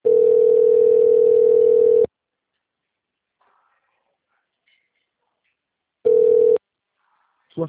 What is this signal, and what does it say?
Ringback tone (North America), 440 + 480 Hz.
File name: out- -2016- - - .79